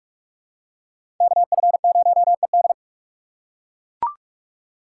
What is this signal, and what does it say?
Morse code, heard as a keyed tone, about 40 wpm, 700 Hz. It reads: KF0ED